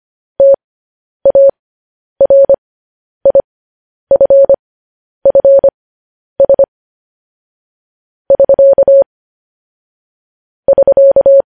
Morse code, heard as a keyed tone, about 25 wpm, spaced as for 10 wpm, 550 Hz. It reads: TARIFFS <SK> <SK>
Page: news headlines in morse code at 10 wpm